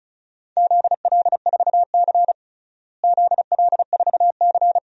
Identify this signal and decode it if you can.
Morse code, heard as a keyed tone, about 36 wpm, 700 Hz. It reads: ZL4C ZL4C
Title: DXCC-100 Call Signs - 4 Characters - Repeated 2x - 35wpm